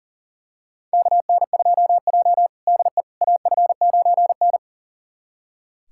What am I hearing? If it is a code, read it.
Morse code, heard as a keyed tone, about 40 wpm, 700 Hz. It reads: KN2J DE AF9N
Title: USA Callsign DE Callsign - 4 Characters - 40wpm